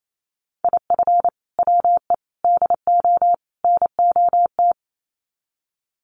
Morse code, heard as a keyed tone, about 28 wpm, 700 Hz. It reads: IF WE DO NOT
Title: Sets of 4 Words 28wpm